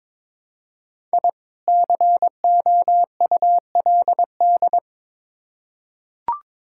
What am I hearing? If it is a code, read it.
Morse code, heard as a keyed tone, about 23 wpm, 700 Hz. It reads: I COULD
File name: Build A Phrase - 2-Word Phrases - 22wpm